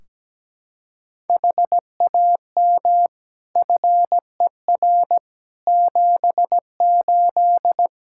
Morse code, heard as a keyed tone, about 17 wpm, 700 Hz. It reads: HAM FER 78
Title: Practice QSO - Exchange 3 - More Info - 17wpm